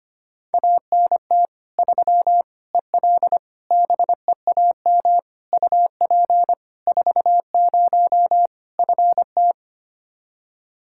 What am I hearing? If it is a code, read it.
Morse code, heard as a keyed tone, about 25 wpm, 700 Hz. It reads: ANT 3 EL BEAM UP 40 FT
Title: Practice QSO - Exchange 2 - Info - 25wpm